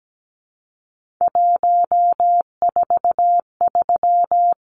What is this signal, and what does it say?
Morse code, heard as a keyed tone, about 17 wpm, 700 Hz. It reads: 143
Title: Numbers - 3-Digit 17wpm